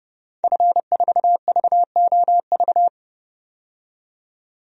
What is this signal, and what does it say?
Morse code, heard as a keyed tone, about 30 wpm, 700 Hz. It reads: F4VOV
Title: DXCC-100 Call Signs - 5 Characters - 30wpm